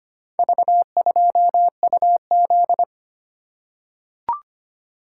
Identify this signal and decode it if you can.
Morse code, heard as a keyed tone, about 25 wpm, 700 Hz. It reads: V2UZ
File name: practice / 25wpm / Four Letter-Number Combinations - 25wpm